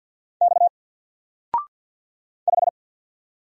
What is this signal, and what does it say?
Morse code, heard as a keyed tone, about 50 wpm, 700 Hz. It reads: X 5
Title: Single Letter-Number Alt 50wpm